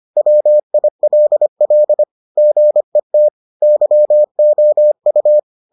Morse code, heard as a keyed tone, about 25 wpm, 600 Hz. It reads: WILL GET YOU